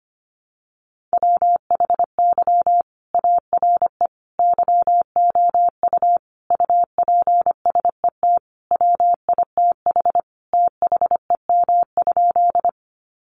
Morse code, heard as a keyed tone, about 25 wpm, 700 Hz. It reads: WHY ARE YOU UPSET WITH THEM?
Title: Sentences from Top 800 Words 25wpm